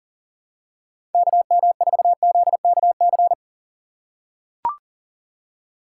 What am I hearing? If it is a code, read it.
Morse code, heard as a keyed tone, about 40 wpm, 700 Hz. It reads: KM4ZKC
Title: USA Call Signs - 6 Characters - 3x Character Spacing - 40wpm